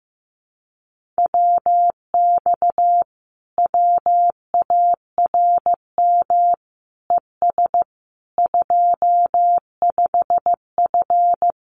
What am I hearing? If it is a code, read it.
Morse code, heard as a keyed tone, about 15 wpm, 700 Hz. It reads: WX WARM ES 25F
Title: Common QSO Phrases Encore 15wpm